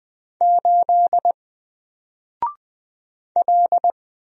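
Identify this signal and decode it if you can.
Morse code, heard as a keyed tone, about 20 wpm, 700 Hz. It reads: 8 L
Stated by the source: Single letter and number Morse drill at 20 wpm, from N4LCD.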